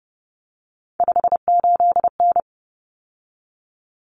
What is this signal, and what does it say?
Morse code, heard as a keyed tone, about 30 wpm, 700 Hz. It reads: 58N
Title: Common Cut Numbers - 3-Digit 30wpm